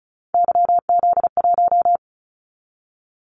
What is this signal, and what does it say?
Morse code, heard as a keyed tone, about 34 wpm, 700 Hz. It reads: YZ1
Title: Callsign Prefixes for Top 25 DXCC - 35wpm